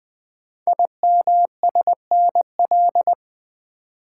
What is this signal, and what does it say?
Morse code, heard as a keyed tone, about 21 wpm, 700 Hz. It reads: IMSNL
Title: Code Groups - 5 Letters - 20wpm